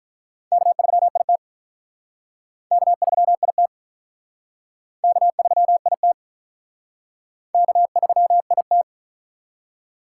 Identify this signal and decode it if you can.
Morse code, heard as a keyed tone, about 47 wpm, 700 Hz. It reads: K3IT K3IT K3IT K3IT